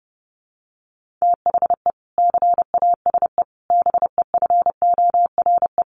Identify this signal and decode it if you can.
Morse code, heard as a keyed tone, about 30 wpm, 700 Hz. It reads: THE CASE BEFORE
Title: Sentences from Top 1400 Words 30wpm